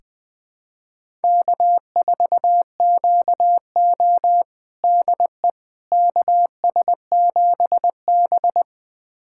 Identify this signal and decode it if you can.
Morse code, heard as a keyed tone, about 20 wpm, 700 Hz. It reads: K4QO DE KS7B
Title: USA Callsign DE Callsign - 4 Characters - 20wpm